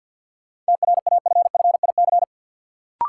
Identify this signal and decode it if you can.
Morse code, heard as a keyed tone, about 53 wpm, 700 Hz. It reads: TRAFFIC
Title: Sentences from Top 1300 Words - New Word Drill Alt - 50wpm